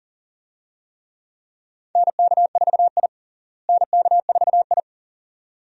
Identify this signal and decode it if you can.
Morse code, heard as a keyed tone, about 40 wpm, 700 Hz. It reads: NK4I NK4I
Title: USA Call Signs - 4 Characters - Repeated 2x - 40wpm